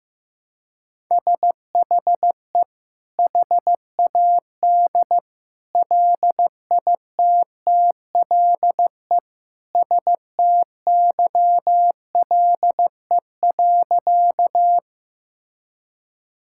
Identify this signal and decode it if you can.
Morse code, heard as a keyed tone, about 15 wpm, 700 Hz. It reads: SHE HAD LITTLE STYLE.
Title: Sentences from Top 1000 Words - Review Part 5 - 15wpm